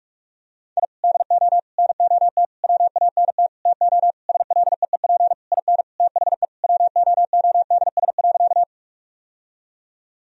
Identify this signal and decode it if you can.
Morse code, heard as a keyed tone, about 46 wpm, 700 Hz. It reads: I DO NOT WANT TO SLEEP IN THE WOODS.
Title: Sentences from Top 1000 Words - Review Part 2 - 45wpm